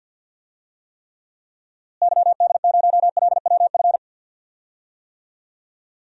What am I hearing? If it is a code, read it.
Morse code, heard as a keyed tone, about 46 wpm, 700 Hz. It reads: YD0LWF
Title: DXCC-100 Call Signs - 6 Characters - Repeated 3x - 50wpm